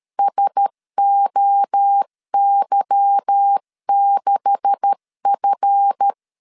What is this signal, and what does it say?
Morse code, heard as a keyed tone, about 13 wpm, 800 Hz. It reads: SOY6F
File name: LEZIONI / 9 286 12wpm 2305